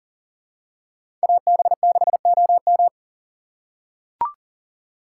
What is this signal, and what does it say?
Morse code, heard as a keyed tone, about 40 wpm, 700 Hz. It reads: AB6OM
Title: USA Call Signs - 3x Character Spacing - 40wpm